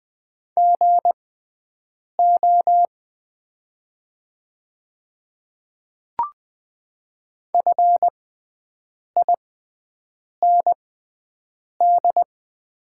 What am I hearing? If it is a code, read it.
Morse code, heard as a keyed tone, about 20 wpm, spaced as for 7 wpm, 700 Hz. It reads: GO FIND
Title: 6x Character Spacing - Top 100 Words - 20wpm